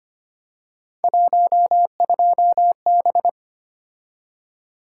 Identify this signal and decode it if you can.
Morse code, heard as a keyed tone, about 25 wpm, 700 Hz. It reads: 12B
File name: Three Letter-Number Combinations - 25wpm